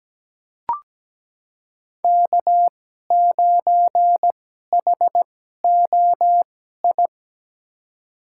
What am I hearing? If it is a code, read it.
Morse code, heard as a keyed tone, about 17 wpm, spaced as for 12 wpm, 700 Hz. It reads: K9HOI
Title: USA Call Signs - 5 Characters - 2x Character Spacing - 17wpm